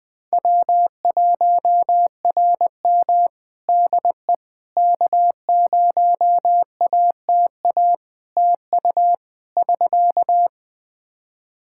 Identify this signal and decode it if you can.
Morse code, heard as a keyed tone, about 20 wpm, 700 Hz. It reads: W1RM DE K0ATA TU <SK>
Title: Practice QSO - Exchange 4 - Ending - 20wpm